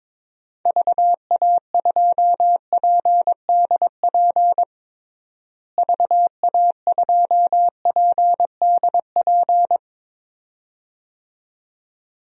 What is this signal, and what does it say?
Morse code, heard as a keyed tone, about 22 wpm, 700 Hz. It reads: VA2PDP VA2PDP
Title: DXCC-100 Call Signs - 6 Characters - Repeated 2x - 22wpm